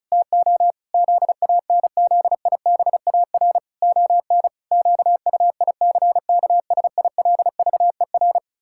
Morse code, heard as a keyed tone, about 34 wpm, 700 Hz. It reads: TO ZANZIBAR ON QUICKSILVER
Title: Pangram Challenge v2 - 35wpm